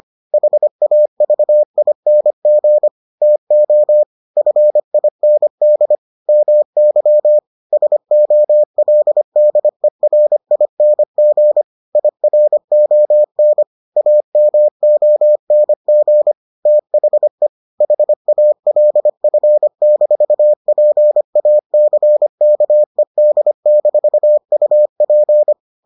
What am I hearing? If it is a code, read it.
Morse code, heard as a keyed tone, about 25 wpm, 600 Hz. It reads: HAVING TO FIND MY SOLDERING IRON AMONG THE HALF-PACKED-UP